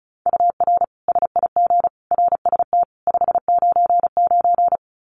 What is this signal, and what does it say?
Morse code, heard as a keyed tone, about 34 wpm, 700 Hz. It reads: UR SIG RST 599
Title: Common QSO Phrases 35wpm